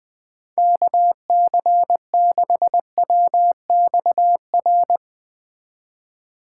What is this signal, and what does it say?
Morse code, heard as a keyed tone, about 21 wpm, 700 Hz. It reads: KC6WXR